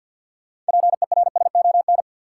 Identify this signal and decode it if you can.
Morse code, heard as a keyed tone, about 53 wpm, 700 Hz. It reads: PERSON
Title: Sentences from Top 400 Words - New Word Drill - 50wpm